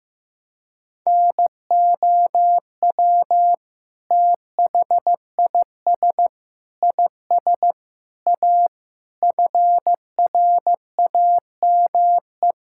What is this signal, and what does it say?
Morse code, heard as a keyed tone, about 15 wpm, 700 Hz. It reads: NOW THIS IS A FRAME
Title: Sentences from Top 1000 Words - Review Part 6 - 15wpm